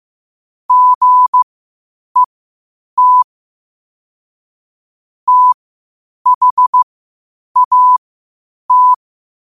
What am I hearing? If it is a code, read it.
Morse code, heard as a keyed tone, about 15 wpm, spaced as for 8 wpm, 1000 Hz. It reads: GET THAT
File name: qotdfiles / 8wpm